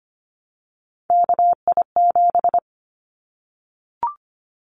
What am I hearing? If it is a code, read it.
Morse code, heard as a keyed tone, about 25 wpm, 700 Hz. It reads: KI7
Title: Callsign Prefixes for Top 25 DXCC - Repeated 3x - 25wpm